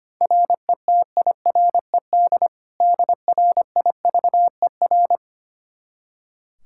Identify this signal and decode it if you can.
Morse code, heard as a keyed tone, about 25 wpm, 700 Hz. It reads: RETIRED DRIVER
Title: K1USN SST QSO Practice 25wpm